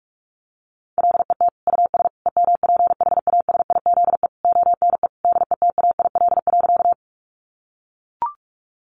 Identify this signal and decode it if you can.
Morse code, heard as a keyed tone, about 44 wpm, 700 Hz. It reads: LET US EMPHASIZE ONE DETAIL.
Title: Sentences from Top 1500 Words 45wpm